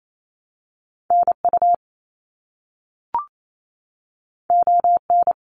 Morse code, heard as a keyed tone, about 28 wpm, 700 Hz. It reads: NU ON
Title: Canadian Provinces and Territories 28wpm